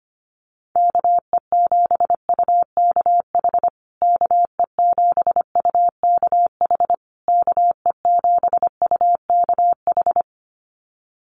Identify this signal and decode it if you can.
Morse code, heard as a keyed tone, about 25 wpm, 700 Hz. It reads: KE7UKH KE7UKH KE7UKH
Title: USA Call Signs Repeated 3x 25wpm